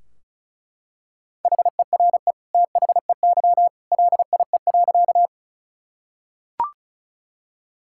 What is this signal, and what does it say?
Morse code, heard as a keyed tone, about 36 wpm, 700 Hz. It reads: HERE THEY LIE.
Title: Sentences from Top 1600 Words 35wpm